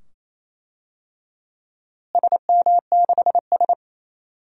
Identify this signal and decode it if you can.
Morse code, heard as a keyed tone, about 28 wpm, 700 Hz. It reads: SM6S
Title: DXCC-100 Call Signs - 4 Characters - Repeated 2x - 28wpm